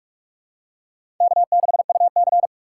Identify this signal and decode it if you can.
Morse code, heard as a keyed tone, about 41 wpm, 700 Hz. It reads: K6UC
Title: USA Call Signs - 4 Characters - 4x Character Spacing - 45wpm